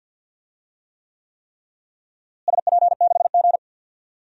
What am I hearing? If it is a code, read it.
Morse code, heard as a keyed tone, about 53 wpm, 700 Hz. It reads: SP6G
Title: DXCC-100 Call Signs - 4 Characters - Repeated 3x - 50wpm